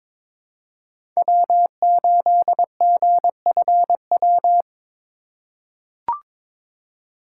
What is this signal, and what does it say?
Morse code, heard as a keyed tone, about 22 wpm, 700 Hz. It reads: W8GFW
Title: USA Call Signs - 5 Characters - 2x Character Spacing - 22wpm